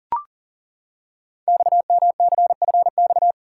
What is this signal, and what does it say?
Morse code, heard as a keyed tone, about 40 wpm, 700 Hz. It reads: XMCFX
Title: Code Groups - 5 Letters - 40wpm